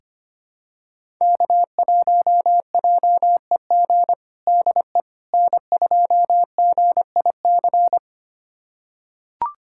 Morse code, heard as a keyed tone, about 25 wpm, 700 Hz. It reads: K1JEG DE N2GIC
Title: USA Callsign DE Callsign 25wpm